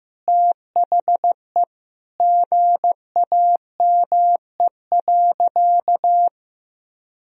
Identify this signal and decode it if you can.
Morse code, heard as a keyed tone, about 15 wpm, 700 Hz. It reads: THE GAME.
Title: Sentences from Top 1000 Words - Review Part 4 - 15wpm